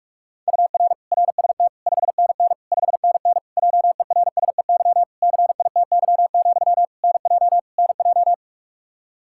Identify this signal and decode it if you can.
Morse code, heard as a keyed tone, about 47 wpm, 700 Hz. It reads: UR RST 5NN 5NN JERSEY CITY, NJ NJ
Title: Practice QSO - Exchange 1 - Beginning - 45wpm